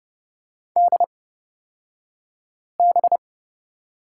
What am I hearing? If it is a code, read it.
Morse code, heard as a keyed tone, about 30 wpm, spaced as for 5 wpm, 690 Hz. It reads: DB
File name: Lesson 36d - BT - Two Characters 30wpm